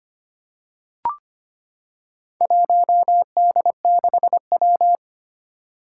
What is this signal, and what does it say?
Morse code, heard as a keyed tone, about 25 wpm, 700 Hz. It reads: 1D6W
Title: Four Challenging Letter-Number Combinations 25wpm